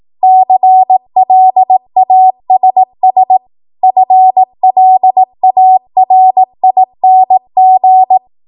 Morse code, heard as a keyed tone, about 18 wpm, 750 Hz. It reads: CLASS FLARING
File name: Morse / 18WPM